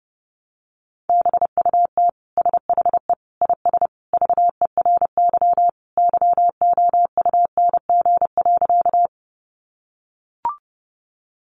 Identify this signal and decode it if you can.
Morse code, heard as a keyed tone, about 28 wpm, 700 Hz. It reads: BUT SHE IS VERY YOUNG.
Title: Sentences from Top 400 Words 30wpm